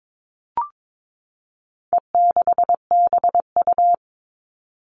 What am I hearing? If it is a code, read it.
Morse code, heard as a keyed tone, about 22 wpm, 700 Hz. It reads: E6BU
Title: All Cut Numbers - 4-Digit 22wpm